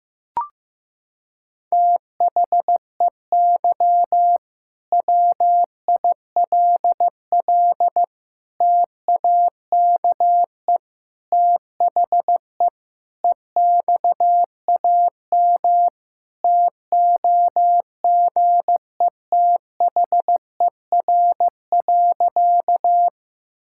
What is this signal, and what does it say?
Morse code, heard as a keyed tone, about 16 wpm, 700 Hz. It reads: THEY WILL TAKE THE EXAM TOGETHER.
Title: Sentences from Top 1000 Words - Review Part 5 - 15wpm